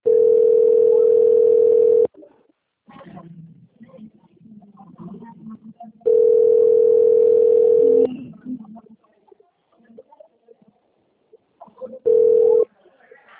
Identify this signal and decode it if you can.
Ringback tone (North America), 440 + 480 Hz, with speech.
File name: out- - - - - .38096